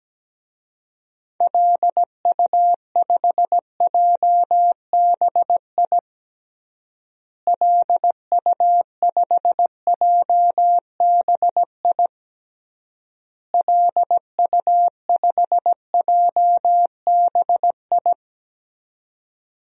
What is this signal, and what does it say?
Morse code, heard as a keyed tone, about 17 wpm, 700 Hz. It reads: LU5JBI LU5JBI LU5JBI
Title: DXCC-100 Call Signs - 6 Characters - Repeated 3x - 17wpm